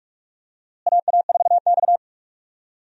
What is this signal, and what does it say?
Morse code, heard as a keyed tone, about 43 wpm, 700 Hz. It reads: AA4X
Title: USA Call Signs - 4 Characters - Repeated 3x - 45wpm